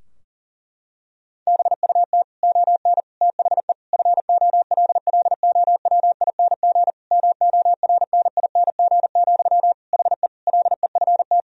Morse code, heard as a keyed tone, about 37 wpm, 700 Hz. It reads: BUT ON THE FOLLOWING MORNING, HE LEFT
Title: Sentences from Top 1000 Words - Review Part 4 - 40wpm